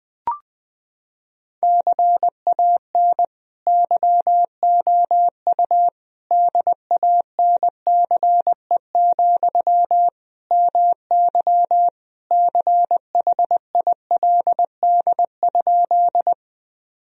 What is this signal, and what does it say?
Morse code, heard as a keyed tone, about 20 wpm, 700 Hz. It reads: CAN YOU DANCE, MY CHILD?
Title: Sentences from Top 1000 Words - Review Part 4 - 20wpm